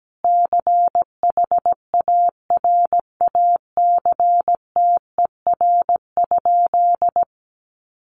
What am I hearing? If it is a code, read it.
Morse code, heard as a keyed tone, about 17 wpm, 700 Hz. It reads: CHARACTER?
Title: Sentences from Top 1200 Words 17wpm